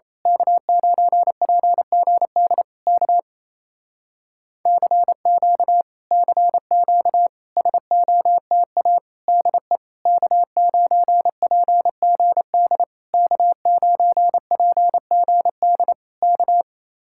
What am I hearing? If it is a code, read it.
Morse code, heard as a keyed tone, about 27 wpm, 700 Hz. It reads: K9PGD K CQ CQ SOTA DE K9PGD K9PGD K
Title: Speed-Racing Common QSO Phrases Encore - 28wpm